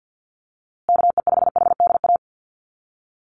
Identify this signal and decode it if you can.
Morse code, heard as a keyed tone, about 50 wpm, 700 Hz. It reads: KE5HDA
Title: USA Call Signs Repeated 2x 50wpm